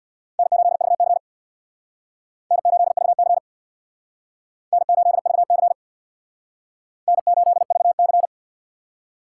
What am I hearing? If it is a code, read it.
Morse code, heard as a keyed tone, about 64 wpm, 700 Hz. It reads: N8VC N8VC N8VC N8VC